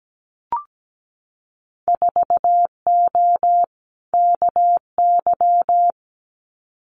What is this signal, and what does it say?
Morse code, heard as a keyed tone, about 17 wpm, 700 Hz. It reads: 4O KY
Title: Winter Field Day Exchanges Encore 17wpm